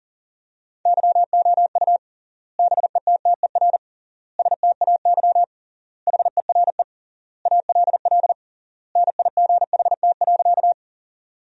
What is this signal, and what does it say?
Morse code, heard as a keyed tone, about 40 wpm, 700 Hz. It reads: YOU BETTER STAY HERE ALL NIGHT.
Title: 3x Word-Spacing - Sentences from Top 500 Words - 40wpm